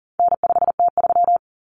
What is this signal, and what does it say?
Morse code, heard as a keyed tone, about 40 wpm, 700 Hz. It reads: N5T3
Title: Common Cut Numbers - 4-Digit 40wpm